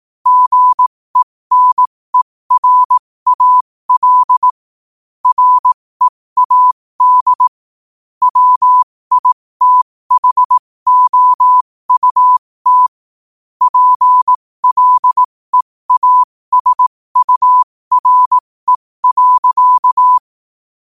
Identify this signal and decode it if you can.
Morse code, heard as a keyed tone, about 18 wpm, spaced as for 15 wpm, 1000 Hz. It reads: GENERAL READ WITHOUT PLEASURE.
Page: New quotes every day in morse code at 15 Words per minute.